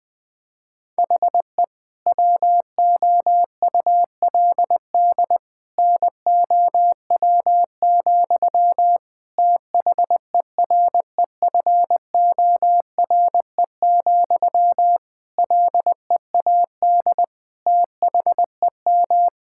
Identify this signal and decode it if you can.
Morse code, heard as a keyed tone, about 20 wpm, 700 Hz. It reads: HE WOULD NOW, THEREFORE, LEAD THEM
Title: Sentences from Top 1000 Words - Review Part 5 - 20wpm